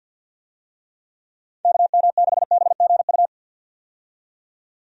Morse code, heard as a keyed tone, about 49 wpm, 700 Hz. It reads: KM6BGU
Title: USA Call Signs - 6 Characters - 50wpm